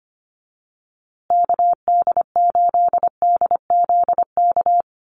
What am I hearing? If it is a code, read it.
Morse code, heard as a keyed tone, about 25 wpm, 700 Hz. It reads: KD8DZK
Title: USA Call Signs Repeated 2x 25wpm